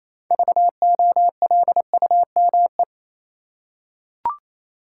Morse code, heard as a keyed tone, about 28 wpm, 700 Hz. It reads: VOLUME
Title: Six Letter Words - 28wpm